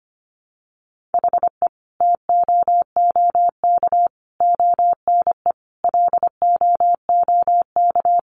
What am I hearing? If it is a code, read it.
Morse code, heard as a keyed tone, about 25 wpm, 700 Hz. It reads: HE TOOK ONE LOOK
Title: Sentences from Top 600 Words 25wpm